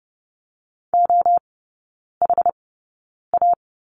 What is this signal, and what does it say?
Morse code, heard as a keyed tone, about 30 wpm, spaced as for 9 wpm, 700 Hz. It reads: OHA
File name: Lesson 18e - Letter m - Three Characters 30wpm